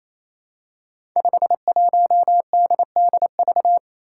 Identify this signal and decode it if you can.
Morse code, heard as a keyed tone, about 29 wpm, 700 Hz. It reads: 51DDV
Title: Five Challenging Letter-Number Combinations 28wpm